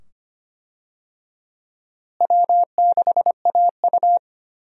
Morse code, heard as a keyed tone, about 25 wpm, 700 Hz. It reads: W6AU
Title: USA Call Signs - 4 Characters - Repeated 2x - 25wpm